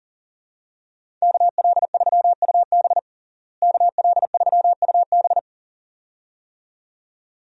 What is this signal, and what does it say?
Morse code, heard as a keyed tone, about 40 wpm, 700 Hz. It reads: KL3UB KL3UB
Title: USA Call Signs - 5 Characters - Repeated 2x - 40wpm